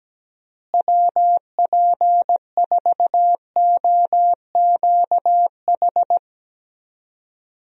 Morse code, heard as a keyed tone, about 17 wpm, 700 Hz. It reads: WP4OQH
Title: USA Call Signs - 6 Characters - Repeated 2x - 17wpm